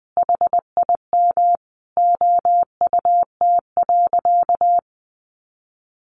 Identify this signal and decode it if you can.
Morse code, heard as a keyed tone, about 20 wpm, 700 Hz. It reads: HIM OUT.
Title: Sentences from Top 1500 Words 20wpm